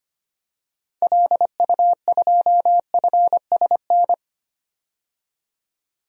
Morse code, heard as a keyed tone, about 25 wpm, 700 Hz. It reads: LU2FSN
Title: DXCC-100 Call Signs - 6 Characters - 25wpm